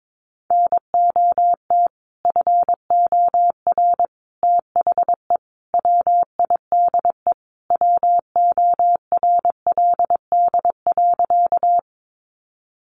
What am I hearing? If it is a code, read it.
Morse code, heard as a keyed tone, about 22 wpm, 700 Hz. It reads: NOT FOR THE WIDE WORLD.
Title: Sentences from Top 900 Words 22wpm